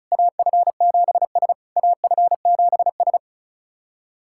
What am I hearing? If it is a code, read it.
Morse code, heard as a keyed tone, about 36 wpm, 700 Hz. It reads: AF7S AF7S
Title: SOTA QSO Practice 35wpm